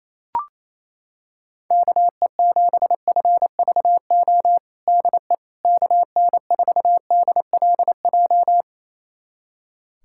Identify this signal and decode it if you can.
Morse code, heard as a keyed tone, about 28 wpm, 700 Hz. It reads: KE7FVO DE KN4DLJ
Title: USA Callsign DE Callsign - 6 Characters - 28wpm